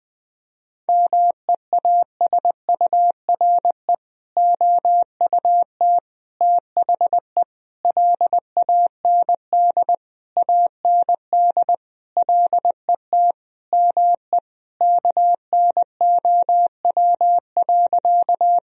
Morse code, heard as a keyed tone, about 20 wpm, 700 Hz. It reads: MEASURE OUT THE LAND AND LET ME KNOW.
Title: Sentences from Top 1000 Words - Review Part 2 - 20wpm